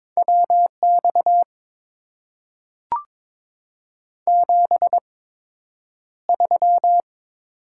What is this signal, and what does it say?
Morse code, heard as a keyed tone, about 22 wpm, spaced as for 9 wpm, 700 Hz. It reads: WX 73
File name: Top 90 QSO Elements - 8x Character Spacing - 22wpm